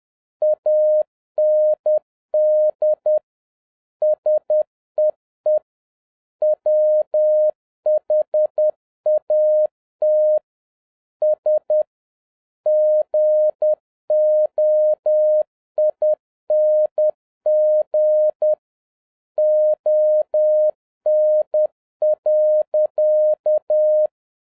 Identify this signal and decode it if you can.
Morse code, heard as a keyed tone, about 10 wpm, 610 Hz. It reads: AND SEE WHAT S GOING ON.
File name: 2011-07-mp3 / n31 THE 3.11 2 10wpm